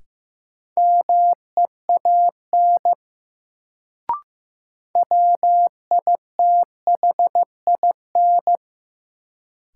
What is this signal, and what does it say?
Morse code, heard as a keyed tone, about 15 wpm, 700 Hz. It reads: MEAN WITHIN
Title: Sets of 2 Words - New Word Drill - 15wpm